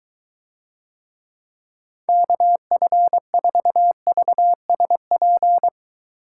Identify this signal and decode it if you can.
Morse code, heard as a keyed tone, about 22 wpm, 700 Hz. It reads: KF4VSP